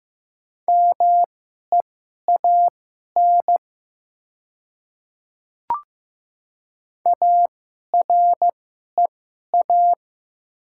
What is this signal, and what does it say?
Morse code, heard as a keyed tone, about 15 wpm, spaced as for 11 wpm, 700 Hz. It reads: MEAN AREA
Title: 2x Character Spacing - Top 500 Words - 15wpm